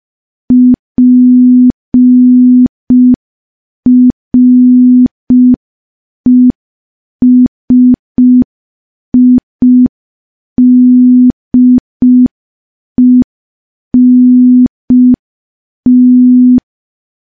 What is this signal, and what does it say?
Morse code, heard as a keyed tone, about 5 wpm, 250 Hz. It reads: PRESIDENT
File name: morse / morse-training-05WPM